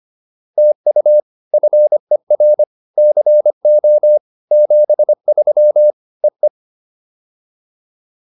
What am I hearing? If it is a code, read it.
Morse code, heard as a keyed tone, about 25 wpm, 600 Hz. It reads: TU FER CO 73 EE
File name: POTA307a 25